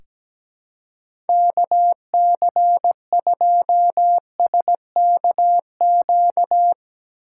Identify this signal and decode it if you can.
Morse code, heard as a keyed tone, about 17 wpm, 700 Hz. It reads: KC2SKQ
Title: USA Call Signs - 6 Characters - 4x Character Spacing - 17wpm